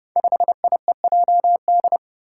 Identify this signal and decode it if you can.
Morse code, heard as a keyed tone, about 30 wpm, 700 Hz. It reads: 5IEJD
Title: Five Challenging Letter-Number Combinations 30wpm